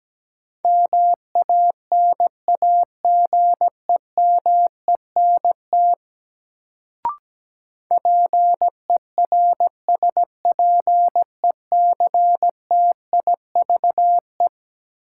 Morse code, heard as a keyed tone, about 17 wpm, 700 Hz. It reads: MANAGEMENT PERSPECTIVE
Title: Sentences from Top 1200 Words - New Word Drill Alt - 17wpm